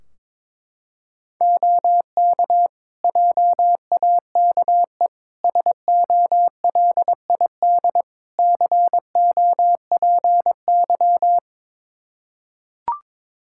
Morse code, heard as a keyed tone, about 22 wpm, 700 Hz. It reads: OK JAKE SOLID COPY
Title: Practice QSO - Exchange 3 - More Info - 22wpm